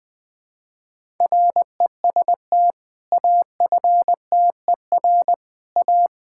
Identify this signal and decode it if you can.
Morse code, heard as a keyed tone, about 20 wpm, 700 Hz. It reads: REST AFTER A
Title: Sentences with 2-Word Phrases - 20wpm